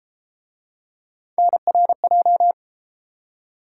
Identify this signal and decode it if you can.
Morse code, heard as a keyed tone, about 31 wpm, 700 Hz. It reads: NRJ